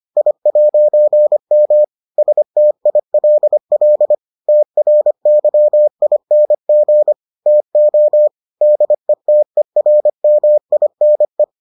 Morse code, heard as a keyed tone, about 25 wpm, 600 Hz. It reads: I'M STILL TRYING TO DETERMINE